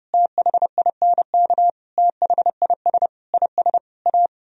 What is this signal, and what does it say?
Morse code, heard as a keyed tone, about 38 wpm, 700 Hz. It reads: THINK THIS IS A